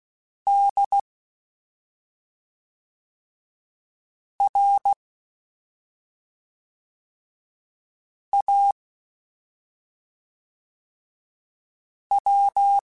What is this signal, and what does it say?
Morse code, heard as a keyed tone, about 16 wpm, 800 Hz. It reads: D R A W
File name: Special INGs / Special INGs-mp3s / SlowToFaster ing @5